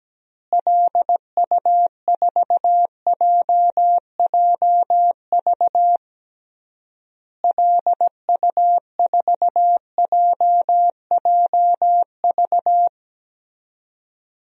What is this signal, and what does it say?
Morse code, heard as a keyed tone, about 17 wpm, 700 Hz. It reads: LU4JJV LU4JJV
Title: DXCC-100 Call Signs - 6 Characters - Repeated 3x - 17wpm